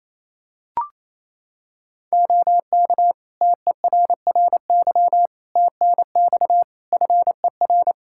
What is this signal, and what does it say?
Morse code, heard as a keyed tone, about 27 wpm, 700 Hz. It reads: OK TERRY TNX FER
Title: Practice QSO - Exchange 4 - Ending - 28wpm